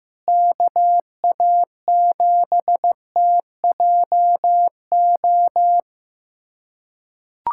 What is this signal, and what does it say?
Morse code, heard as a keyed tone, about 15 wpm, 700 Hz. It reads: KA7TJO
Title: USA Callsign DE Callsign - 6 Characters - 15wpm